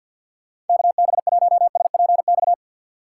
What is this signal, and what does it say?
Morse code, heard as a keyed tone, about 52 wpm, 690 Hz. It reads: KB1SPX
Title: USA Call Signs - 2x Character Spacing - 50wpm